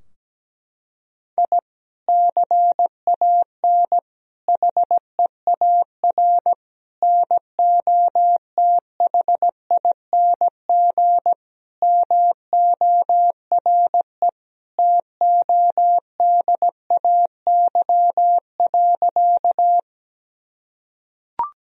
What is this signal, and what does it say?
Morse code, heard as a keyed tone, about 17 wpm, 700 Hz. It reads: I CAN HEAR NOTHING MORE TODAY.
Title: Sentences from Top 1000 Words - Review Part 4 - 17wpm